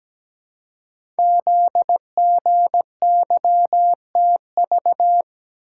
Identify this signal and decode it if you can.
Morse code, heard as a keyed tone, about 17 wpm, 700 Hz. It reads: ZGYTV
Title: Code Groups - 5 Letters - 17wpm